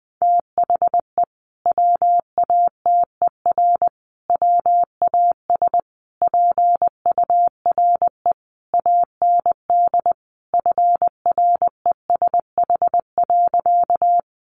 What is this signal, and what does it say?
Morse code, heard as a keyed tone, about 20 wpm, 700 Hz. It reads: THE WATER WAS PURE AND FRESH.
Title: Sentences from Top 1500 Words 20wpm